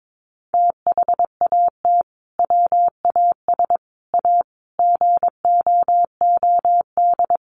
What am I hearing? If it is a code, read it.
Morse code, heard as a keyed tone, about 22 wpm, 700 Hz. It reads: THAT WAS A GOOD
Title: Sentences from Top 900 Words 22wpm